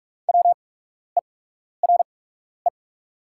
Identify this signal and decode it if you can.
Morse code, heard as a keyed tone, about 44 wpm, spaced as for 12 wpm, 700 Hz. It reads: WERE
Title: 8x Character Spacing - Top 200 Words - 45wpm